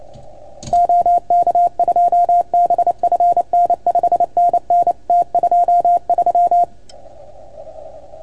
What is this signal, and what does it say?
Morse code, heard as a keyed tone, about 28 wpm, 690 Hz. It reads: OK2BFN 5NN T23
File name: Archiv / CUC310 / OK1FGD CUC 310 pileup